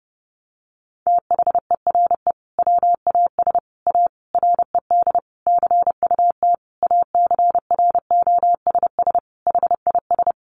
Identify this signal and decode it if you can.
Morse code, heard as a keyed tone, about 30 wpm, 700 Hz. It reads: THERE WAS A RED CUT ACROSS HIS
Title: Sentences from Top 700 Words 30wpm